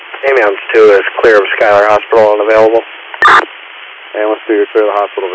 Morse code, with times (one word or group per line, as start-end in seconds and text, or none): none